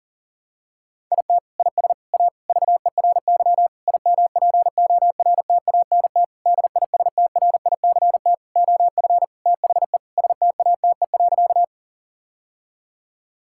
Morse code, heard as a keyed tone, about 37 wpm, 700 Hz. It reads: IT IS A VERY IMPORTANT DISTRICT OF THE STATE.
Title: Sentences from Top 1600 Words 40wpm